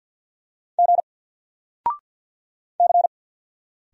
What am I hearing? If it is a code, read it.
Morse code, heard as a keyed tone, about 50 wpm, 700 Hz. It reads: G C
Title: Single Letter-Number 50wpm